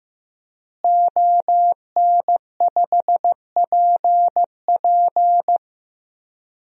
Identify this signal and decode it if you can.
Morse code, heard as a keyed tone, about 15 wpm, 700 Hz. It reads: ON5PP
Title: DXCC-100 Call Signs - 5 Characters - Repeated 2x - 15wpm